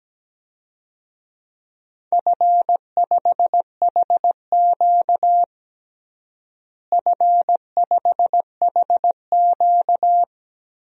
Morse code, heard as a keyed tone, about 17 wpm, 700 Hz. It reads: F5HQ F5HQ
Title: DXCC-100 Call Signs - 4 Characters - Repeated 2x - 17wpm